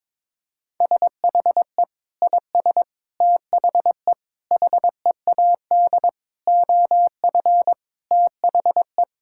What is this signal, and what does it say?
Morse code, heard as a keyed tone, about 22 wpm, 700 Hz. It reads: SHE IS THE HEAD OF THE
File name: practice / 22wpm / Sentences with 3-Word Phrases - 22wpm